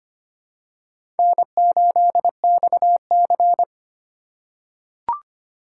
Morse code, heard as a keyed tone, about 25 wpm, 700 Hz. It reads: N8XC
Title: USA Call Signs - 3x Character Spacing - 25wpm